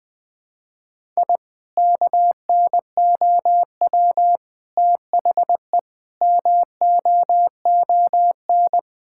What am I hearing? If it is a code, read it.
Morse code, heard as a keyed tone, about 21 wpm, 700 Hz. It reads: I KNOW THE MOON
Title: Sentences from Top 1000 Words - Review Part 6 - 20wpm